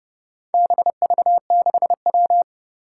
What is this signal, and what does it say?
Morse code, heard as a keyed tone, about 30 wpm, 700 Hz. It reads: BV6W
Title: Four Challenging Letter-Number Combinations 30wpm